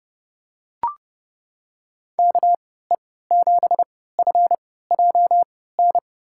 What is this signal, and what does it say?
Morse code, heard as a keyed tone, about 30 wpm, spaced as for 17 wpm, 700 Hz. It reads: KE7FJN
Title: USA Call Signs - 6 Characters - 3x Character Spacing - 30wpm